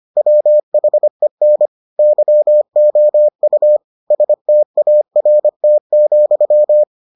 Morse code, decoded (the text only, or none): WHEN YOU START,